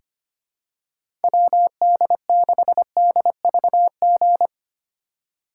Morse code, heard as a keyed tone, about 25 wpm, 700 Hz. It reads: WD6DVG
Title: USA Call Signs - 3x Character Spacing - 25wpm